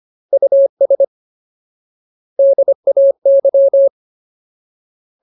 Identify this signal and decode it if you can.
Morse code, heard as a keyed tone, about 25 wpm, 550 Hz. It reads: US DAY
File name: Ultimate Word List / 25WPM Ultimate